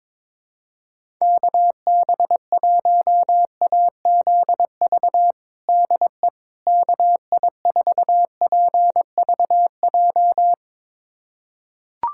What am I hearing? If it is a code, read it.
Morse code, heard as a keyed tone, about 21 wpm, 700 Hz. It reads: KB1AZV DE KI4PVJ
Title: USA Callsign DE Callsign - 6 Characters - 22wpm